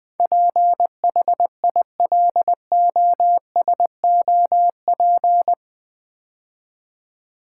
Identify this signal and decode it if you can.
Morse code, heard as a keyed tone, about 20 wpm, 700 Hz. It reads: PHILOSOP
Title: Build A Word - 4 Letters At A Time - 11-Letter Words - 20wpm